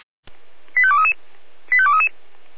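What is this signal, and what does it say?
Five-tone selective call (ZVEI), heard as quick runs of five tones, tones of about 70 ms. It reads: 85233 85233
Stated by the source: SDR-received ZVEI five-tone call.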